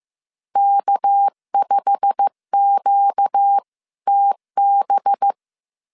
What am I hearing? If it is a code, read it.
Morse code, heard as a keyed tone, about 15 wpm, 800 Hz. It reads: K5QTB
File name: CW / DET / 1 9 1500C S C 14wpm 3245